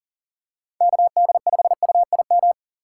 Morse code, heard as a keyed tone, about 40 wpm, 700 Hz. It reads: KD5UIM
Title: USA Call Signs - 6 Characters - 2x Character Spacing - 40wpm